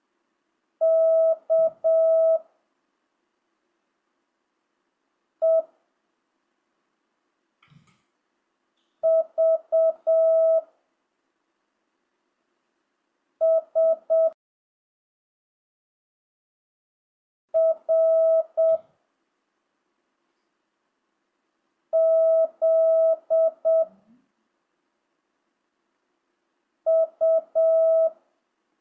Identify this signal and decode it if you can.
Morse code, heard as a keyed tone, about 7 wpm, 1300 Hz. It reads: K E V S R Z U
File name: jh8cbh / 2025cw